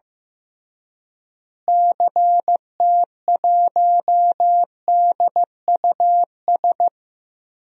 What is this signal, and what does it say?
Morse code, heard as a keyed tone, about 15 wpm, 700 Hz. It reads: CT1DUS
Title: DXCC-100 Call Signs - 6 Characters - Repeated 2x - 15wpm